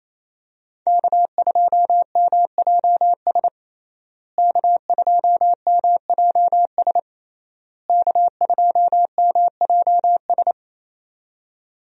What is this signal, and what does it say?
Morse code, heard as a keyed tone, about 28 wpm, 700 Hz. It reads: K2MJS K2MJS K2MJS
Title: USA Call Signs - 5 Characters - Repeated 3x - 28wpm